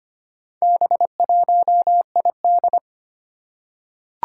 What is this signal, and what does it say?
Morse code, heard as a keyed tone, about 25 wpm, 700 Hz. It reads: B1ID
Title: Four Challenging Letter-Number Combinations 25wpm